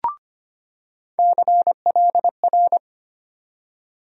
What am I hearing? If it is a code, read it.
Morse code, heard as a keyed tone, about 25 wpm, 700 Hz. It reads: CLR
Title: USA Call Sign Postfixes - 3 Letters 25wpm